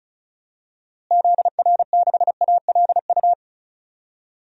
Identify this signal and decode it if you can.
Morse code, heard as a keyed tone, about 36 wpm, 700 Hz. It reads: ZR6ALU
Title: DXCC-100 Call Signs - 6 Characters - Repeated 3x - 35wpm